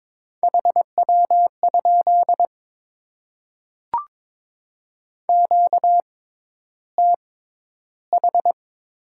Morse code, heard as a keyed tone, about 22 wpm, spaced as for 11 wpm, 690 Hz. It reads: HW? QTH
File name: Top 20 QSO Elements - 6x Character Spacing - 22wpm